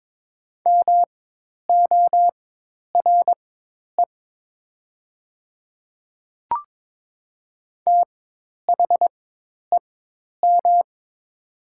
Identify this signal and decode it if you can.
Morse code, heard as a keyed tone, about 22 wpm, spaced as for 10 wpm, 700 Hz. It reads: MORE THEM
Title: 4x Character Spacing - Top 500 Words - 22wpm